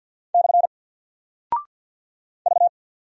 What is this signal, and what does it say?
Morse code, heard as a keyed tone, about 50 wpm, 700 Hz. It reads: / V
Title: Single Character (Letter:Number:Punctuation:Prosign) 50wpm